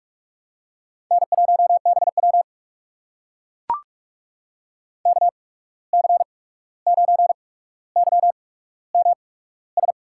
Morse code, heard as a keyed tone, about 44 wpm, spaced as for 17 wpm, 690 Hz. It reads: N1BW KC9YMS
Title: USA Call Signs - 8x Character Spacing - 45wpm